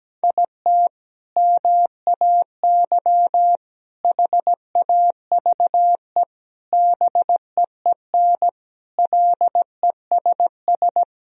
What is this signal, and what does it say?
Morse code, heard as a keyed tone, about 17 wpm, 700 Hz. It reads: IT MAY HAVE BEEN LESS
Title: Sentences from Top 1000 Words - Review Part 2 - 17wpm